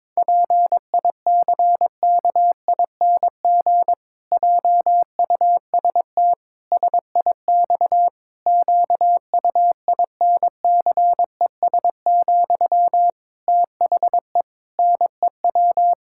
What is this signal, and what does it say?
Morse code, heard as a keyed tone, about 22 wpm, 700 Hz. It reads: PICKING JUST SIX QUINCES, THE NEW
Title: Pangram Challenge v2 - 22wpm